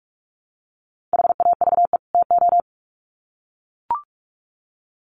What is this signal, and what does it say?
Morse code, heard as a keyed tone, about 44 wpm, 700 Hz. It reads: HAVE TO
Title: Sets of 2 Words 45wpm